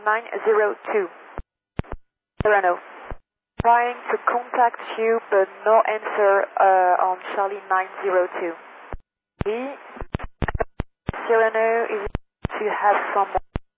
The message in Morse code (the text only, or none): none